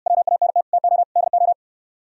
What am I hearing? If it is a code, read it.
Morse code, heard as a keyed tone, about 68 wpm, 700 Hz. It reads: WANT TO DO